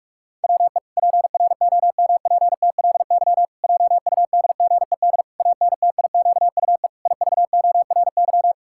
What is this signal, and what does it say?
Morse code, heard as a keyed tone, about 46 wpm, 700 Hz. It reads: WE PROMPTLY JUDGED ANTIQUE IVORY
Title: Pangram Challenge v2 - 45wpm